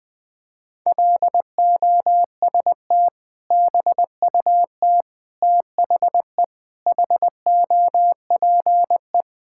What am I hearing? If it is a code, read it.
Morse code, heard as a keyed tone, about 20 wpm, 700 Hz. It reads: LOST BUT THE HOPE
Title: Sentences with 2-Word Phrases - 20wpm